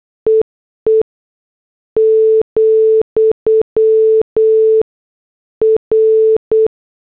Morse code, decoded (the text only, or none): EE , R